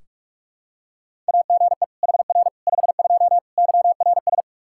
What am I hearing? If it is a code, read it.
Morse code, heard as a keyed tone, about 46 wpm, 700 Hz. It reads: AGE HR 52 YRS
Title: Practice QSO - Exchange 3 - More Info - 45wpm